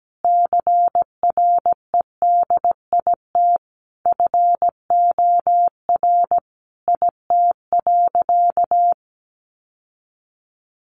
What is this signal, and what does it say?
Morse code, heard as a keyed tone, about 17 wpm, 700 Hz. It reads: CREDIT FOR IT.
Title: Sentences from Top 1000 Words 17wpm